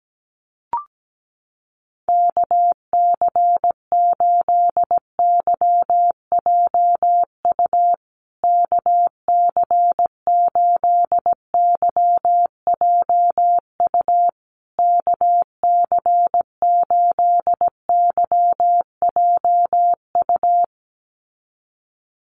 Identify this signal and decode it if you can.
Morse code, heard as a keyed tone, about 17 wpm, 700 Hz. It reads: KC8YJU KC8YJU KC8YJU
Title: USA Call Signs Repeated 3x 17wpm